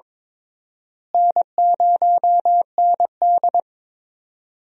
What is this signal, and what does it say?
Morse code, heard as a keyed tone, about 23 wpm, 700 Hz. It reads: N0ND
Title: USA Call Signs - 4 Characters - Repeated 3x - 22wpm